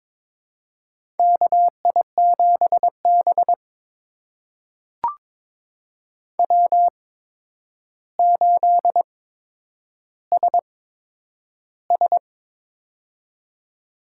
Morse code, heard as a keyed tone, about 22 wpm, spaced as for 9 wpm, 700 Hz. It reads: KI7B W8SS
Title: USA Call Signs - 4 Characters - 8x Character Spacing - 22wpm